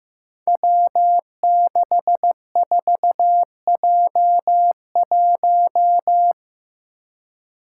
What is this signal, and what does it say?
Morse code, heard as a keyed tone, about 15 wpm, 700 Hz. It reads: W64J1
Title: Five Challenging Letter-Number Combinations 15wpm